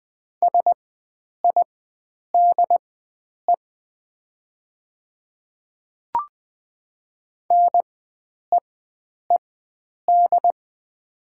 Morse code, heard as a keyed tone, about 20 wpm, spaced as for 9 wpm, 690 Hz. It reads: SIDE NEED
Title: 4x Character Spacing - Top 300 Words - 20wpm